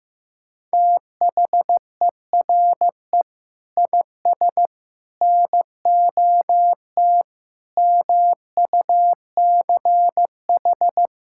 Morse code, heard as a keyed tone, about 15 wpm, 700 Hz. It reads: THERE IS NOT MUCH
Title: Sentences from Top 1600 Words 15wpm